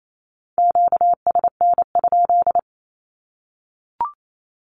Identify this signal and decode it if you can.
Morse code, heard as a keyed tone, about 28 wpm, 700 Hz. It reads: QSN?
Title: Q-Codes 28wpm